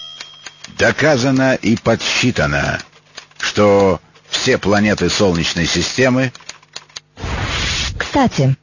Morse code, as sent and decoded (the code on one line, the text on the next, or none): none